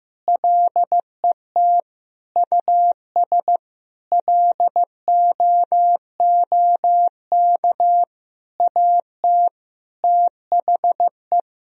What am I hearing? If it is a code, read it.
Morse code, heard as a keyed tone, about 15 wpm, 700 Hz. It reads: LET US LOOK AT THE
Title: Sentences from Top 1000 Words - Review Part 3 - 15wpm